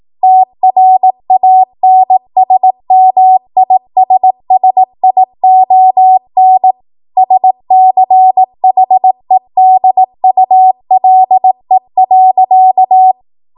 Morse code, heard as a keyed tone, about 18 wpm, 750 Hz. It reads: TRANSMISSION SCHEDULE.
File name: Archive / 18 WPM / 18WPM